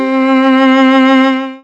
There is no code.